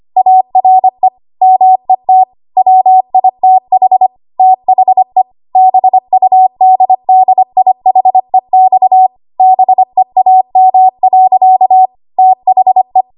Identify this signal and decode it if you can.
Morse code, heard as a keyed tone, about 25 wpm, 750 Hz. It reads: ARE MET WITH THE BUDDIHEX BEAM. THE
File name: Archive / 25 WPM / 25WPM